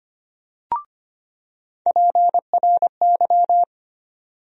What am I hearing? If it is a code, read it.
Morse code, heard as a keyed tone, about 25 wpm, 700 Hz. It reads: PRY
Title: USA Call Sign Postfixes - 3 Letters 25wpm